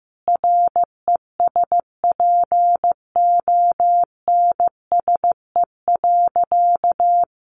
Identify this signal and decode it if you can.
Morse code, heard as a keyed tone, about 15 wpm, 700 Hz. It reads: RESPONSE.
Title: Sentences from Top 1400 Words 15wpm